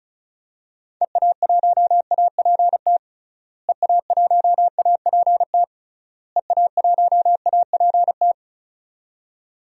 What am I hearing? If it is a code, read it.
Morse code, heard as a keyed tone, about 34 wpm, 700 Hz. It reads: EA1APT EA1APT EA1APT
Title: DXCC-100 Call Signs - 6 Characters - Repeated 3x - 35wpm